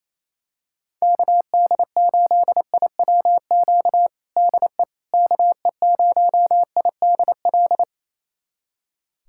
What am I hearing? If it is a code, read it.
Morse code, heard as a keyed tone, about 27 wpm, 700 Hz. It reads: KD8IWQ DE KE0IDL
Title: USA Callsign DE Callsign - 6 Characters - 28wpm